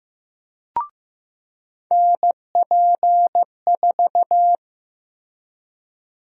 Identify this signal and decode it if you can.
Morse code, heard as a keyed tone, about 16 wpm, 700 Hz. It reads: NP4
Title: USA Call Sign Prefixes - Repeated 2x - 15wpm